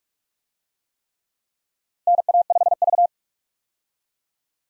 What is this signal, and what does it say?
Morse code, heard as a keyed tone, about 46 wpm, 700 Hz. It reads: NA5V
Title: USA Call Signs - 4 Characters - 45wpm